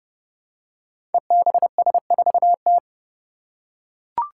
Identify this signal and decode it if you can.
Morse code, heard as a keyed tone, about 32 wpm, 700 Hz. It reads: EBS4T
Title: Five Challenging Letter-Number Combinations 30wpm